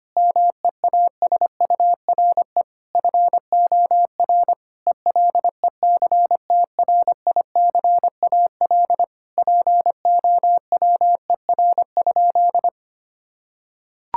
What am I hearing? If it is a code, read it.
Morse code, heard as a keyed tone, about 25 wpm, 700 Hz. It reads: MEASURE FOR ELECTRICAL POWER?
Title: Sentences from Top 1000 Words - Review Part 6 - 25wpm